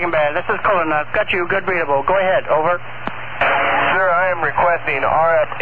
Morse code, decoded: none

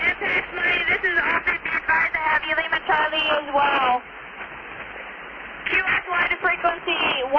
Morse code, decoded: none

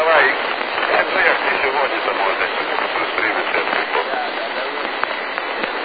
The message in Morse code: none

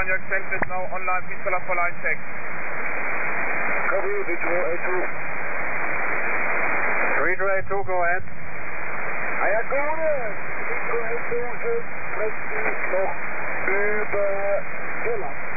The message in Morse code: none

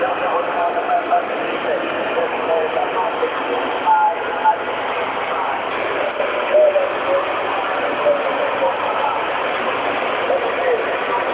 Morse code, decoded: none

